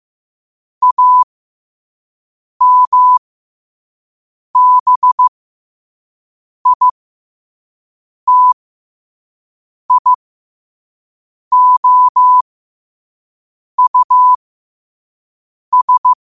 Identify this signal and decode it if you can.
Morse code, heard as a keyed tone, about 15 wpm, spaced as for 5 wpm, 1000 Hz. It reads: AMBITIOUS
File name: qotdfiles / 5wpm